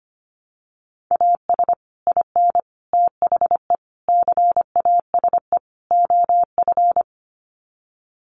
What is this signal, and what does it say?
Morse code, heard as a keyed tone, about 25 wpm, 700 Hz. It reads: AS IN THE CASE OF
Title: Sets of 5 Words 25wpm